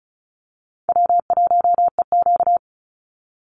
Morse code, heard as a keyed tone, about 34 wpm, 700 Hz. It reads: W1EQ
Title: Easy Call Signs Repeated 2x - 35wpm